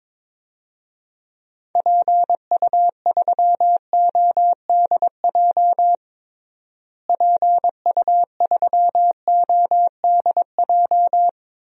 Morse code, heard as a keyed tone, about 22 wpm, 700 Hz. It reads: PU3ODJ PU3ODJ
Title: DXCC-100 Call Signs - 6 Characters - Repeated 3x - 22wpm